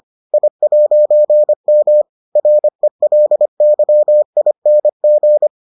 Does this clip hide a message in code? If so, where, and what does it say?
Yes: Morse code from 0.3 to 5.5 s, heard as a keyed tone, about 25 wpm, 600 Hz. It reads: I'M RELYING